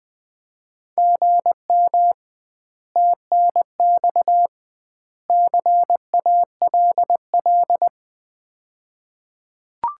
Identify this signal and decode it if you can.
Morse code, heard as a keyed tone, about 21 wpm, 700 Hz. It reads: GM TNX CALL
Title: Common QSO Phrases - 2x Word-Spacing - 20wpm